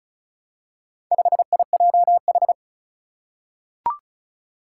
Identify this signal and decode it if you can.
Morse code, heard as a keyed tone, about 36 wpm, 700 Hz. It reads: 5IJH
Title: Four Challenging Letter-Number Combinations 35wpm